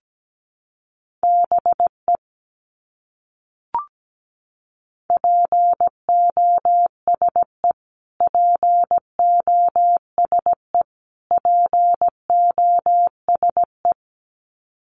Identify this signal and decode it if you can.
Morse code, heard as a keyed tone, about 17 wpm, 700 Hz. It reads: BE POSE POSE POSE
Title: Top 500 Words Repeated 17wpm